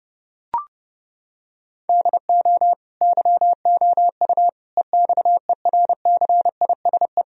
Morse code, heard as a keyed tone, about 30 wpm, 700 Hz. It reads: DO YOU EXERCISE
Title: Sentences from Top 1000 Words - Review Part 4 - 30wpm